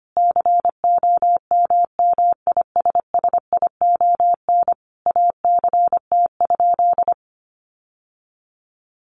Morse code, heard as a keyed tone, about 25 wpm, 700 Hz. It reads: COMMISSION ACT?
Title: Sentences from Top 1400 Words 25wpm